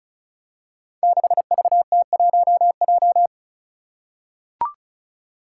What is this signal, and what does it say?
Morse code, heard as a keyed tone, about 36 wpm, 700 Hz. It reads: 6VT1J
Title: Five Challenging Letter-Number Combinations 35wpm